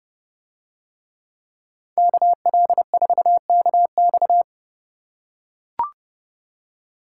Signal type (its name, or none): Morse code